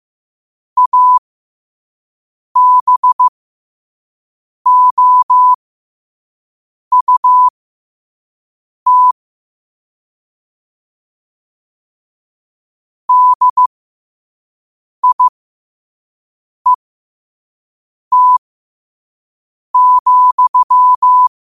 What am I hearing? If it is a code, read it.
Morse code, heard as a keyed tone, about 15 wpm, spaced as for 5 wpm, 1000 Hz. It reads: ABOUT DIET,
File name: qotdfiles / 5wpm